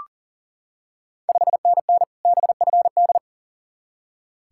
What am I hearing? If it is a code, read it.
Morse code, heard as a keyed tone, about 40 wpm, 700 Hz. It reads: 5NN BFD
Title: Ontario QSO Party Exchange 40wpm